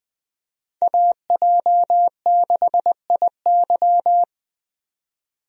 Morse code, heard as a keyed tone, about 20 wpm, 700 Hz. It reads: AJ6IY
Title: USA Call Signs - 5 Characters - 20wpm